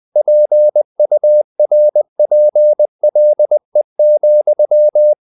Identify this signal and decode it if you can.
Morse code, heard as a keyed tone, about 20 wpm, 600 Hz. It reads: PURPLE,
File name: the old man and the sea / the old man and the sea part 1 20wpm